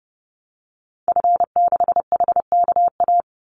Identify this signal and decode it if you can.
Morse code, heard as a keyed tone, about 30 wpm, 700 Hz. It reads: F6HKA
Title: Call Signs Repeated 2x - 30wpm